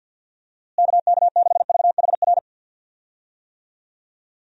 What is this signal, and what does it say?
Morse code, heard as a keyed tone, about 52 wpm, 700 Hz. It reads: KK6VHR
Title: USA Call Signs - 6 Characters - Repeated 3x - 50wpm